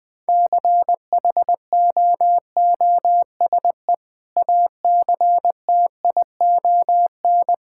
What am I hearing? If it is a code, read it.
Morse code, heard as a keyed tone, about 20 wpm, 700 Hz. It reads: CHOOSE ACTION
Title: Sentences with 2-Word Phrases - 20wpm